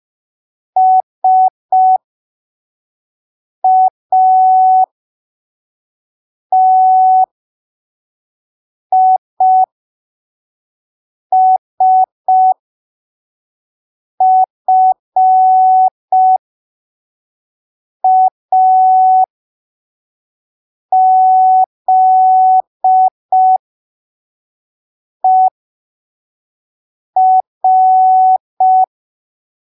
Morse code, heard as a keyed tone, about 5 wpm, spaced as for 3 wpm, 750 Hz. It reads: SATISFAZER
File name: arquivos / texto / texto 23 05ppm